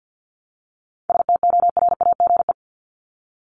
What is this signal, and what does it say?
Morse code, heard as a keyed tone, about 50 wpm, 700 Hz. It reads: STORAGE
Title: Top 1251-1500 Words 50wpm